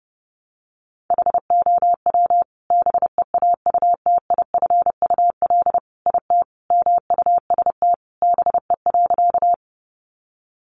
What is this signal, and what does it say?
Morse code, heard as a keyed tone, about 30 wpm, 700 Hz. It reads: HOW BEAUTIFUL IT MUST BE.
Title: Sentences from Top 900 Words 30wpm